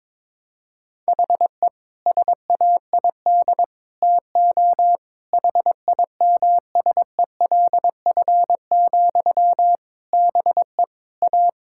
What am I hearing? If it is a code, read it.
Morse code, heard as a keyed tone, about 22 wpm, 700 Hz. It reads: HE SAID TO HIMSELF, BE A
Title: Sentences from Top 1600 Words 22wpm